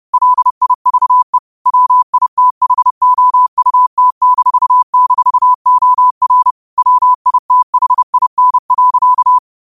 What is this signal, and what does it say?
Morse code, heard as a keyed tone, about 30 wpm, 1000 Hz. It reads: LIVE WITHOUT--OR WITHIN.
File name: qotdfiles / 30wpm